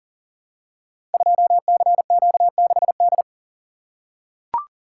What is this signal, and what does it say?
Morse code, heard as a keyed tone, about 41 wpm, 690 Hz. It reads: 2CQ6D
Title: Code Groups - 5 Letter:Numbers - 40wpm